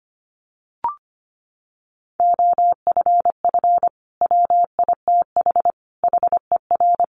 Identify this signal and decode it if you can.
Morse code, heard as a keyed tone, about 25 wpm, 700 Hz. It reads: OFF WITH HER
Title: Sentences from Top 300 Words 25wpm